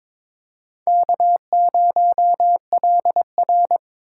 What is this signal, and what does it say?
Morse code, heard as a keyed tone, about 26 wpm, 700 Hz. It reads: K0LR
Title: Speed Racing - USA Call Signs 22wpm